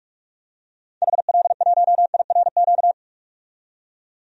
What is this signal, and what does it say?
Morse code, heard as a keyed tone, about 47 wpm, 700 Hz. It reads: HL1IRQ
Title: DXCC-100 Call Signs - 6 Characters - 45wpm